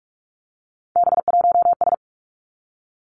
Morse code, heard as a keyed tone, about 46 wpm, 700 Hz. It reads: B1S
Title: Three Challenging Letter-Number Combinations 45wpm